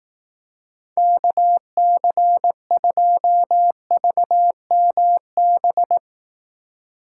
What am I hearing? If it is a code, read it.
Morse code, heard as a keyed tone, about 17 wpm, 700 Hz. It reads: KC2VMB
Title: Speed Racing - USA 2x3 Call Signs 15wpm